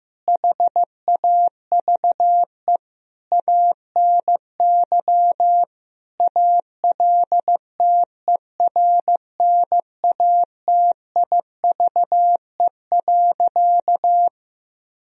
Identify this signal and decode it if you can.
Morse code, heard as a keyed tone, about 14 wpm, 700 Hz. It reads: HAVE ANY ALTERNATIVE.
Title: Sentences from Top 1000 Words - Review Part 4 - 15wpm